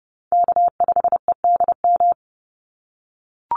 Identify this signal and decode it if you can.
Morse code, heard as a keyed tone, about 30 wpm, 700 Hz. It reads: K5EDM
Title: Call Signs - 30wpm